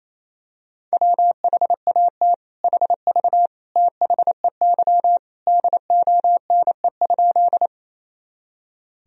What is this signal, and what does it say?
Morse code, heard as a keyed tone, about 28 wpm, 700 Hz. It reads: WHAT HV THEY DONE?
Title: Sentences from Top 500 Words and Abbreviations 28wpm